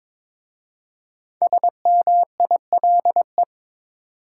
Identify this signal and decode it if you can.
Morse code, heard as a keyed tone, about 22 wpm, 700 Hz. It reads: SMILE
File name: 5 Letter Words - First 3 Letters Given - 22wpm